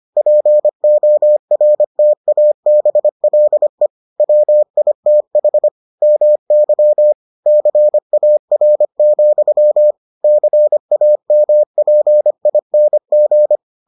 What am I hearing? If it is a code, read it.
Morse code, heard as a keyed tone, about 25 wpm, 600 Hz. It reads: PORTABLE WITH MY CAR, CAMPING